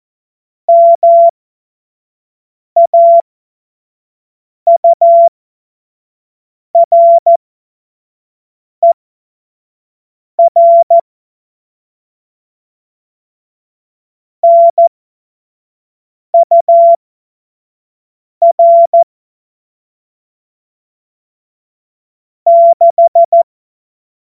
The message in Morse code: MAURER NUR 6

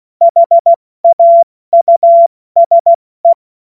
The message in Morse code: HAUSE

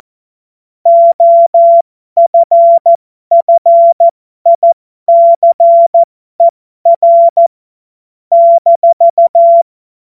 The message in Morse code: OFFICER -

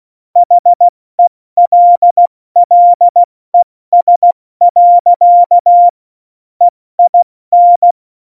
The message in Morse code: HELLES. EIN

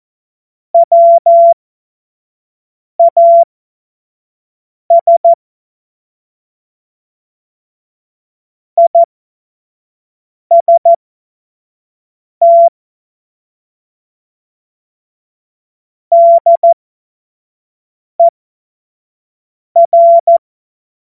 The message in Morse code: WAS IST DER

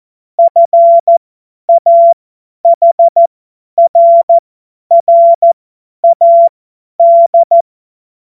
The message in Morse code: FAHRRAD

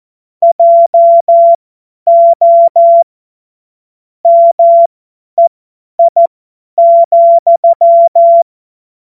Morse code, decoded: JO MEI,

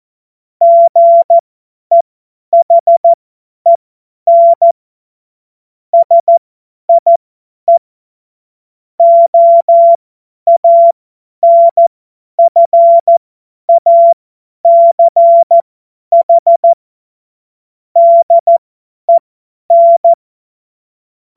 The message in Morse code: GEHEN SIE OANFACH DEN

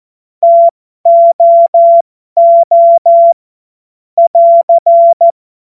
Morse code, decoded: TOO <AR>

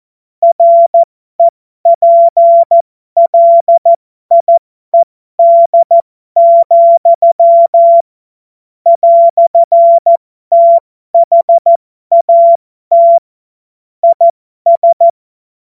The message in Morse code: REPLIED, "THAT IS

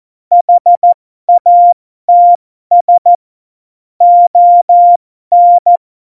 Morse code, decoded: HATS ON

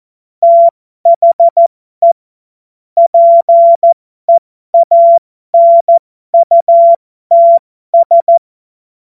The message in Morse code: THE PEANUTS

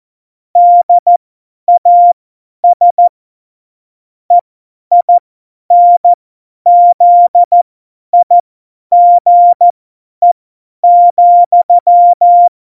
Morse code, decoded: DAS EINZIGE,

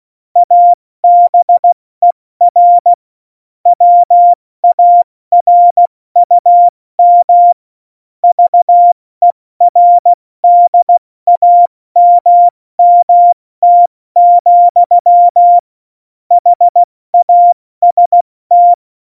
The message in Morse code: ABER WARUM VERDAMMT, HAST

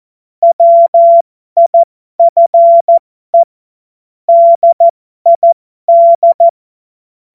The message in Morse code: WIFE DID

